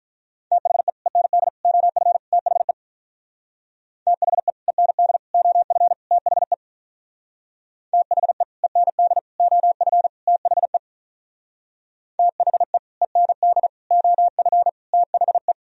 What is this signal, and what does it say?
Morse code, heard as a keyed tone, about 46 wpm, 700 Hz. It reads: THE END OF THE THE END OF THE THE END OF THE THE END OF THE